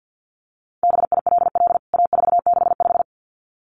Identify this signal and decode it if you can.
Morse code, heard as a keyed tone, about 49 wpm, 700 Hz. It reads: BILL A465
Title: CWT Exchange 50wpm